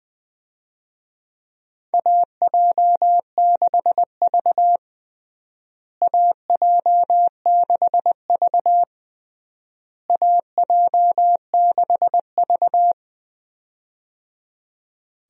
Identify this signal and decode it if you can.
Morse code, heard as a keyed tone, about 20 wpm, 700 Hz. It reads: AJ6V AJ6V AJ6V
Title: USA Call Signs - 4 Characters - Repeated 3x - 20wpm